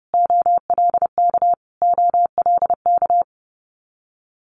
Morse code, heard as a keyed tone, about 30 wpm, 700 Hz. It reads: OLK OLK
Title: Multi-letter Phonemes Repeated 3x 30wpm